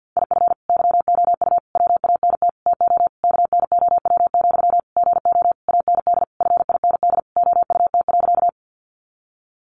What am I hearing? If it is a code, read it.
Morse code, heard as a keyed tone, about 48 wpm, 700 Hz. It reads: IF YOU WANT TO KNOW, GO AND FIND OUT.
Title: Sentences from Top 200 Words 50wpm